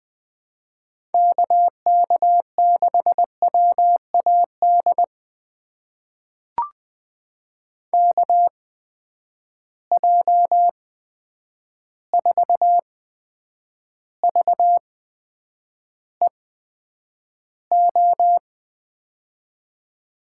Morse code, heard as a keyed tone, about 20 wpm, spaced as for 9 wpm, 700 Hz. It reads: KK6WAD KJ4VEO
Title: USA Call Signs - 6 Characters - 8x Character Spacing - 20wpm